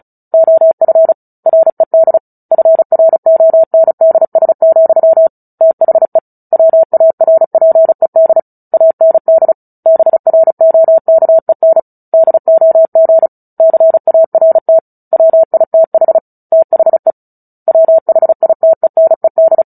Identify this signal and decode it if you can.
Morse code, heard as a keyed tone, about 34 wpm, 650 Hz. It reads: OF RED FRONDS, THE WARPED AND BROKEN DOG CART WITH THE WHITENED
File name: Book Two / 09-Chapter 9 - 35 WPM